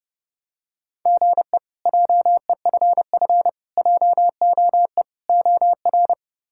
Morse code, heard as a keyed tone, about 30 wpm, 700 Hz. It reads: GE JEFF JOE OR
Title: K1USN SST QSO Practice - 30wpm